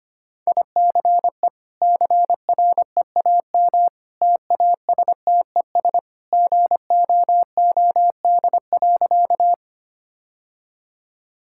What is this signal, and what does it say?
Morse code, heard as a keyed tone, about 25 wpm, 700 Hz. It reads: ICE CREAM TASTES GOOD.
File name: Easy Sentences with Hints - 25wpm